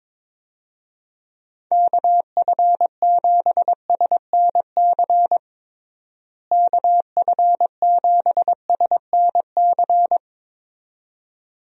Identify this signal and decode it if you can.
Morse code, heard as a keyed tone, about 22 wpm, 700 Hz. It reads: KF7SNC KF7SNC
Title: USA Call Signs - 6 Characters - Repeated 2x - 22wpm